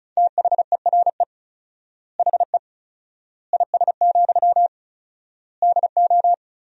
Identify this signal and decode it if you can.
Morse code, heard as a keyed tone, about 34 wpm, 700 Hz. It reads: THERE HE IS, DO
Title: 4x Word-Spacing - Sentences from Top 100 Words - 35wpm